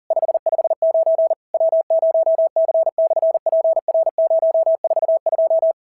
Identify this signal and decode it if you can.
Morse code, heard as a keyed tone, about 40 wpm, 690 Hz. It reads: 559 W0C/PR042
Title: SOTA QSO Practice 40wpm